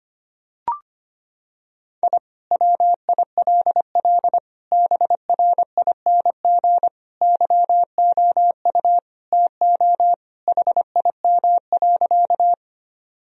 Morse code, heard as a keyed tone, about 25 wpm, 700 Hz. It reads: I WILL BRING YOU TO HIM.
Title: Sentences from Top 1000 Words - Review Part 3 - 25wpm